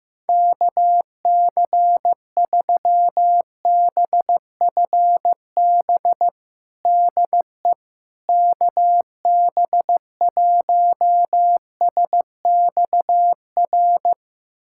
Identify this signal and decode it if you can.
Morse code, heard as a keyed tone, about 15 wpm, 700 Hz. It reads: KC3BFB DE KB1SXR
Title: USA Callsign DE Callsign - 6 Characters - 15wpm